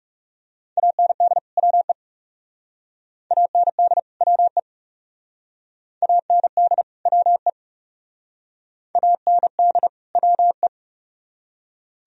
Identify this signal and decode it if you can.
Morse code, heard as a keyed tone, about 36 wpm, 700 Hz. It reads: AND WE AND WE AND WE AND WE
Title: Speed-Racing - Sets of 2 Words 30wpm